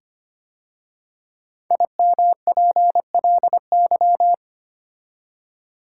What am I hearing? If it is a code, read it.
Morse code, heard as a keyed tone, about 25 wpm, 700 Hz. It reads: IMPLY
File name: 5 Letter Words - First 3 Letters Given - 25wpm